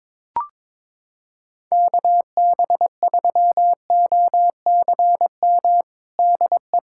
Morse code, heard as a keyed tone, about 22 wpm, 700 Hz. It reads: KB3OCM DE
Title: USA Callsign DE Callsign 22wpm